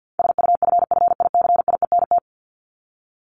Morse code, heard as a keyed tone, about 48 wpm, 700 Hz. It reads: SUFFICIENT
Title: LY Suffix - Top 100 Words with Base Words 50wpm